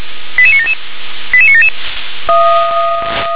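Five-tone selective call (ZVEI), heard as quick runs of five tones, tones of about 70 ms. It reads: 88977 88977
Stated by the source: ZVEI five-tone selective call received by SDR.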